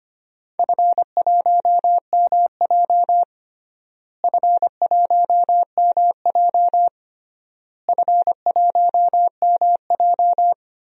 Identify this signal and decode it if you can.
Morse code, heard as a keyed tone, about 25 wpm, 700 Hz. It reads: F1MJ F1MJ F1MJ
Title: DXCC-100 Call Signs - 4 Characters - Repeated 3x - 25wpm